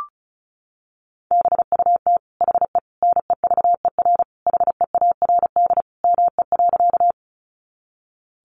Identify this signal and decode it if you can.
Morse code, heard as a keyed tone, about 34 wpm, 700 Hz. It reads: BUT HE NEVER HEARD ME.
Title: Sentences from Top 400 Words 35wpm